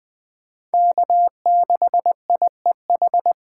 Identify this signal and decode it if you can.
Morse code, heard as a keyed tone, about 20 wpm, 700 Hz. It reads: K6IEH
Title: Speed Racing - USA 1x3 Call Signs 17wpm